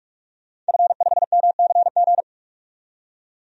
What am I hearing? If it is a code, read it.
Morse code, heard as a keyed tone, about 46 wpm, 700 Hz. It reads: F5MCG
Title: DXCC-100 Call Signs - 5 Characters - 45wpm